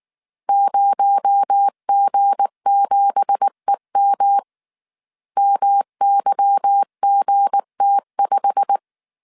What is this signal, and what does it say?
Morse code, heard as a keyed tone, about 19 wpm, 800 Hz. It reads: 0G7EM MYGT5
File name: CW / DET / 1 9 1500C S A 18wpm 2459